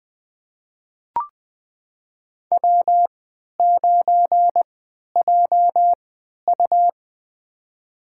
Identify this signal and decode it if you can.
Morse code, heard as a keyed tone, about 20 wpm, spaced as for 11 wpm, 700 Hz. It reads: W9JU
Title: USA Call Signs - 4 Characters - 3x Character Spacing - 20wpm